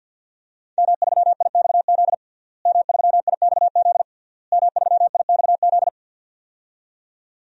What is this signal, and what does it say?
Morse code, heard as a keyed tone, about 52 wpm, 700 Hz. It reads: M3IXZ M3IXZ M3IXZ
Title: DXCC-100 Call Signs - 5 Characters - Repeated 3x - 50wpm